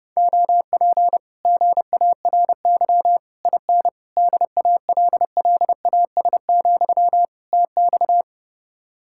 Morse code, heard as a keyed tone, about 32 wpm, 700 Hz. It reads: OP GARY IN DALLAS, TX